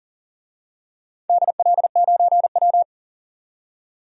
Morse code, heard as a keyed tone, about 40 wpm, 700 Hz. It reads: DL9W
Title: DXCC-100 Call Signs - 4 Characters - 40wpm